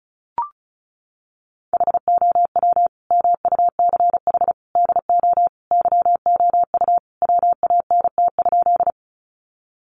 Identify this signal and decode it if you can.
Morse code, heard as a keyed tone, about 34 wpm, 700 Hz. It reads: HOW MUCH DO YOU WANT?
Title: Sentences from Top 200 Words 35wpm